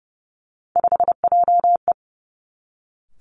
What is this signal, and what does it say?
Morse code, heard as a keyed tone, about 30 wpm, 700 Hz. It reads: 5JE
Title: Three Challenging Letter-Number Combinations 30wpm